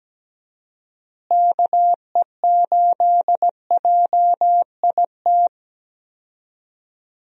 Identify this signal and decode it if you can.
Morse code, heard as a keyed tone, about 17 wpm, 700 Hz. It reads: KE8JIT
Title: USA Call Signs - 6 Characters - Repeated 2x - 17wpm